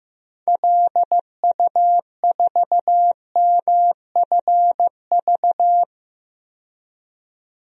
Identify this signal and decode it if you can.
Morse code, heard as a keyed tone, about 15 wpm, 700 Hz. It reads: LU4MFV
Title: DXCC-100 Call Signs - 6 Characters - 15wpm